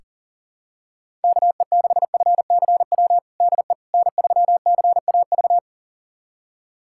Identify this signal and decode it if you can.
Morse code, heard as a keyed tone, about 40 wpm, 700 Hz. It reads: KE6FCW DE N3CAV
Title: USA Callsign DE Callsign 40wpm